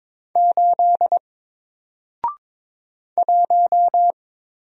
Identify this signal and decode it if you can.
Morse code, heard as a keyed tone, about 22 wpm, 700 Hz. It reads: 8 1
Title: Single Letter-Number Alt 22wpm